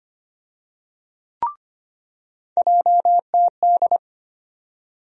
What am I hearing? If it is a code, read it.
Morse code, heard as a keyed tone, about 25 wpm, 700 Hz. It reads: JTD
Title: USA Call Sign Postfixes - 3 Letters 25wpm